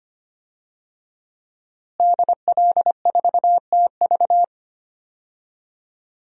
Morse code, heard as a keyed tone, about 25 wpm, 700 Hz. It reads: DL4TV
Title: DXCC-100 Call Signs - 5 Characters - Repeated 3x - 25wpm